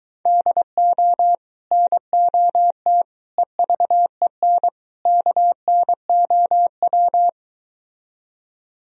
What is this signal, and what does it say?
Morse code, heard as a keyed tone, about 23 wpm, 700 Hz. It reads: DO NOT EVEN KNOW